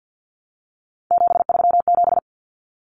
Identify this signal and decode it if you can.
Morse code, heard as a keyed tone, about 50 wpm, 700 Hz. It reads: 737
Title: Numbers - 3-Digit 50wpm